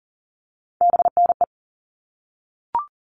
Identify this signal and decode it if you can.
Morse code, heard as a keyed tone, about 40 wpm, 700 Hz. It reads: BNE
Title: All Cut Numbers - 3-Digit 40wpm